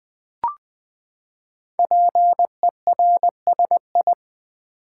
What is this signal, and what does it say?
Morse code, heard as a keyed tone, about 20 wpm, 700 Hz. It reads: PERSI
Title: Build A Word - 5 Letters At A Time - 10-Letter Words - 20wpm